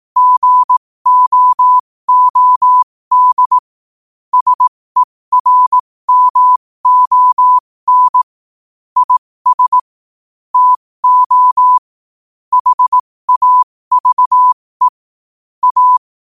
Morse code, heard as a keyed tone, about 18 wpm, spaced as for 15 wpm, 1000 Hz. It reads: GOOD SERMON IS TO HAVE A
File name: qotdfiles / 15wpm